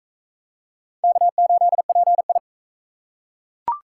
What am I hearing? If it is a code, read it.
Morse code, heard as a keyed tone, about 37 wpm, 700 Hz. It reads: K8PI